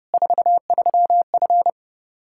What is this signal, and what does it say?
Morse code, heard as a keyed tone, about 30 wpm, 700 Hz. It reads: 43F
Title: Practice QSO - Exchange 2 - Info - 30wpm